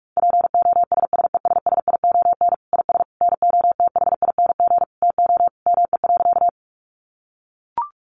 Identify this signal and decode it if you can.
Morse code, heard as a keyed tone, about 44 wpm, 700 Hz. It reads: POSSESSION IS NOTHING TO ME.
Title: Sentences from Top 1400 Words 45wpm